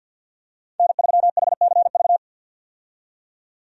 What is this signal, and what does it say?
Morse code, heard as a keyed tone, about 52 wpm, 700 Hz. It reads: N3HCV
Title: USA Call Signs - 5 Characters - 50wpm